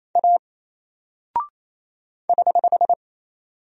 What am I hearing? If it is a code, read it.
Morse code, heard as a keyed tone, about 27 wpm, 700 Hz. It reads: A <HH>
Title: Single Character (Letter:Number:Punctuation:Prosign) 28wpm